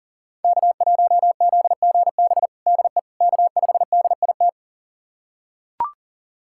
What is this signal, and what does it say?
Morse code, heard as a keyed tone, about 40 wpm, 700 Hz. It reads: K1ZGB DE K5DIT
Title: USA Callsign DE Callsign - 5 Characters - 40wpm